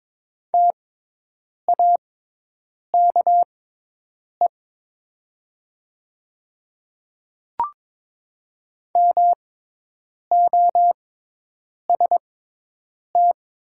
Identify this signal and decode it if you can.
Morse code, heard as a keyed tone, about 22 wpm, spaced as for 8 wpm, 700 Hz. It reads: TAKE MOST
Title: 6x Character Spacing - Top 300 Words - 22wpm